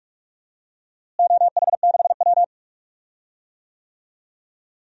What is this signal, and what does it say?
Morse code, heard as a keyed tone, about 45 wpm, 700 Hz. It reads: OH6W
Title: DXCC-100 Call Signs - 4 Characters - Repeated 3x - 45wpm